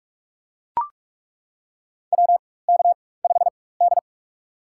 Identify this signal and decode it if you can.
Morse code, heard as a keyed tone, about 41 wpm, spaced as for 20 wpm, 700 Hz. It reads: WK5D
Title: USA Call Signs - 4x Character Spacing - 45wpm